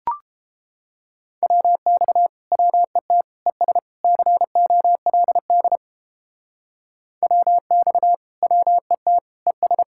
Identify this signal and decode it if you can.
Morse code, heard as a keyed tone, about 31 wpm, 700 Hz. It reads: WX WET ES COLD WX WET ES